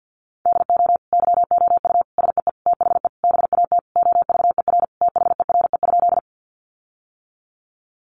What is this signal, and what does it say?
Morse code, heard as a keyed tone, about 50 wpm, 700 Hz. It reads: DO YOU SEE THE BAT OVER THERE?
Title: Sentences from Top 1500 Words 50wpm